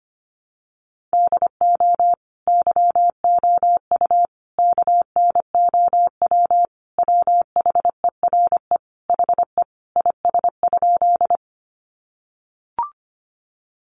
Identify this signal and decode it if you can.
Morse code, heard as a keyed tone, about 25 wpm, 700 Hz. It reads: DO YOU KNOW WHERE HE IS?
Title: Sentences from Top 300 Words 25wpm